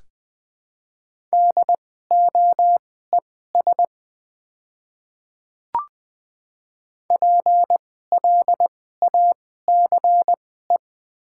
Morse code, heard as a keyed tone, about 20 wpm, spaced as for 14 wpm, 700 Hz. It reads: DOES PLACE
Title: 2x Character Spacing - Top 300 Words - 20wpm